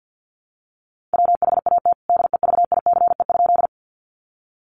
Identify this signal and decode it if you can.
Morse code, heard as a keyed tone, about 48 wpm, 700 Hz. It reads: WHAT DEVICE?
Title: Sentences from Top 1500 Words 50wpm